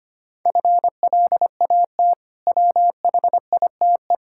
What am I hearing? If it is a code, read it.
Morse code, heard as a keyed tone, about 25 wpm, 700 Hz. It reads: FLAT WHITE
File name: practice / 25wpm / Difficult Words with Hints - 25wpm